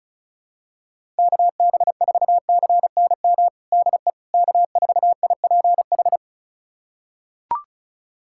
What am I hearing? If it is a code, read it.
Morse code, heard as a keyed tone, about 36 wpm, 700 Hz. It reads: KB4CNM DE K4IPH
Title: USA Callsign DE Callsign 35wpm